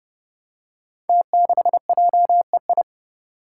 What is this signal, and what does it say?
Morse code, heard as a keyed tone, about 30 wpm, 700 Hz. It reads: T6JEI
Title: Five Challenging Letter-Number Combinations 30wpm